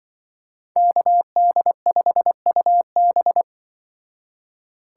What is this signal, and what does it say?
Morse code, heard as a keyed tone, about 23 wpm, 700 Hz. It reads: KD5UB
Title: Speed Racing - USA 2x2 Call Signs 20wpm